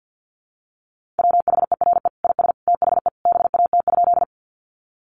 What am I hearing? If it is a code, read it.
Morse code, heard as a keyed tone, about 50 wpm, 700 Hz. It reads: WHERE IS THE BAT?
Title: Sentences from Top 1500 Words 50wpm